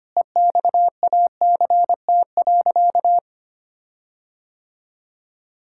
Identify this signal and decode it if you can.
Morse code, heard as a keyed tone, about 25 wpm, 700 Hz. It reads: EXACT.
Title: Sentences from Top 1600 Words 25wpm